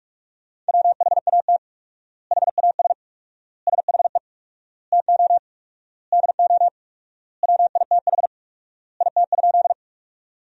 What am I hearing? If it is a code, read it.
Morse code, heard as a keyed tone, about 46 wpm, 700 Hz. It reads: WHAT HAS SHE TO DO WITH IT?
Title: 4x Word-Spacing - Sentences from Top 100 Words - 45wpm